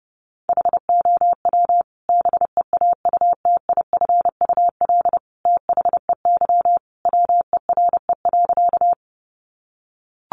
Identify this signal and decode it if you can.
Morse code, heard as a keyed tone, about 30 wpm, 700 Hz. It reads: HOW BEAUTIFUL THEY WERE.
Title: Sentences from Top 900 Words 30wpm